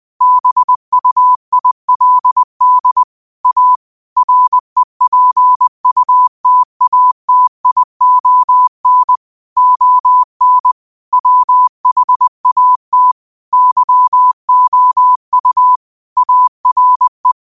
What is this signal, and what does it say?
Morse code, heard as a keyed tone, about 20 wpm, 1000 Hz. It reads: BUILD A REPUTATION ON WHAT YOU ARE
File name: qotdfiles / 20wpm